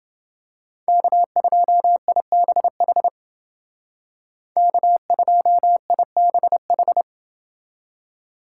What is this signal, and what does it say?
Morse code, heard as a keyed tone, about 30 wpm, 700 Hz. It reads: K2IBH K2IBH